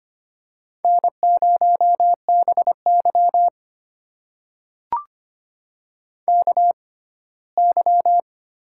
Morse code, heard as a keyed tone, about 25 wpm, spaced as for 17 wpm, 700 Hz. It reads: N0BY KY
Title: USA Call Signs - 4 Characters - 6x Character Spacing - 25wpm